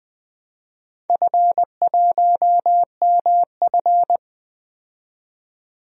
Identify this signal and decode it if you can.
Morse code, heard as a keyed tone, about 20 wpm, 700 Hz. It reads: F1MF
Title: DXCC-100 Call Signs - 4 Characters - 20wpm